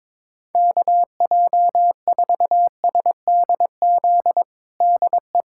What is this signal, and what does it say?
Morse code, heard as a keyed tone, about 22 wpm, 700 Hz. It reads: KJ4SDZ DE
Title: USA Callsign DE Callsign - 6 Characters - 22wpm